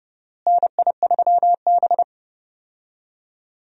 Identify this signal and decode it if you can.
Morse code, heard as a keyed tone, about 30 wpm, 700 Hz. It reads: NI3B
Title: USA Call Signs - 4 Characters - Repeated 2x - 30wpm